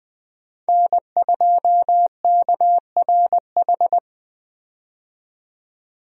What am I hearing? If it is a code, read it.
Morse code, heard as a keyed tone, about 20 wpm, 700 Hz. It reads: N2KRH
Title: USA Call Signs - 5 Characters - 20wpm